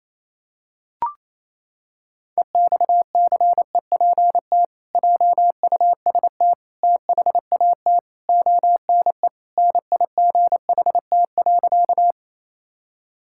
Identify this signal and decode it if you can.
Morse code, heard as a keyed tone, about 28 wpm, 700 Hz. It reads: EXCEPT JUST THAT ONE NIGHT.
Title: Sentences from Top 1600 Words 28wpm